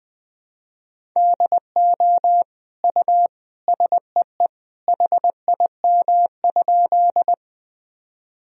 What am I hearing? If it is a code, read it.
Morse code, heard as a keyed tone, about 21 wpm, 700 Hz. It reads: DO U SEE HIM?
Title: Sentences from Top 100 Words and Abbreviations 20wpm